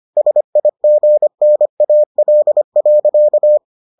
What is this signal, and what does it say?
Morse code, heard as a keyed tone, about 25 wpm, 600 Hz. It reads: SIGNAL.